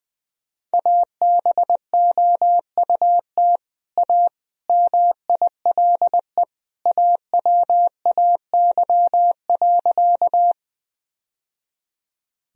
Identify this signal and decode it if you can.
Morse code, heard as a keyed tone, about 20 wpm, 700 Hz. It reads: ABOUT A MILE AWAY.
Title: Sentences from Top 1000 Words - Review Part 1 - 20wpm